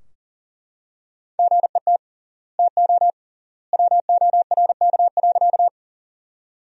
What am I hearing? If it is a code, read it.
Morse code, heard as a keyed tone, about 40 wpm, 700 Hz. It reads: GET TO WORK.
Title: 3x Word-Spacing - Sentences from Top 200 Words - 40wpm